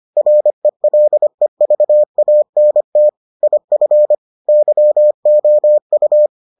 Morse code, heard as a keyed tone, about 25 wpm, 600 Hz. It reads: RELEVANT IF YOU